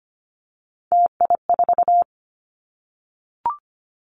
Morse code, heard as a keyed tone, about 25 wpm, 700 Hz. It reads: TI4
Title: Callsign Prefixes for Top 100 DXCC - 25wpm